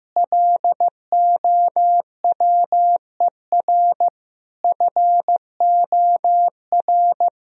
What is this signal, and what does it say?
Morse code, heard as a keyed tone, about 15 wpm, 700 Hz. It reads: LOWER FOR
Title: Sentences from Top 1000 Words - Review Part 4 - 15wpm